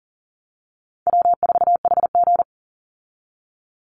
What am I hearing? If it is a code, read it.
Morse code, heard as a keyed tone, about 40 wpm, 700 Hz. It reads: W4HG
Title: Easy Call Signs - 40wpm